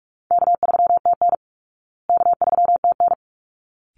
Morse code, heard as a keyed tone, about 44 wpm, 700 Hz. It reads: K3TN K3TN
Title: Easy Call Signs Repeated 2x - 45wpm